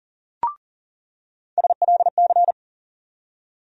Morse code, heard as a keyed tone, about 40 wpm, 700 Hz. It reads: SLC
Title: USA Call Sign Postfixes - 3 Letters 40wpm